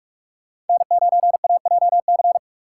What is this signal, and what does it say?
Morse code, heard as a keyed tone, about 46 wpm, 700 Hz. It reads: N9AJC
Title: USA Call Signs - 2x Character Spacing - 45wpm